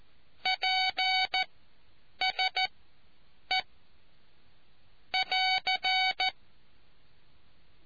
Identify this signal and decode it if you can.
Morse code, heard as a keyed tone, about 14 wpm, spaced as for 8 wpm, 750 Hz. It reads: PSE <AR>